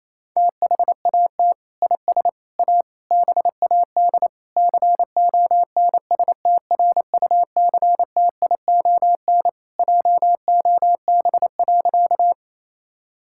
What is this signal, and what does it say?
Morse code, heard as a keyed tone, about 28 wpm, 700 Hz. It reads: THAT IS A BAD CONSTRUCTION JOB.
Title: Sentences from Top 1600 Words 28wpm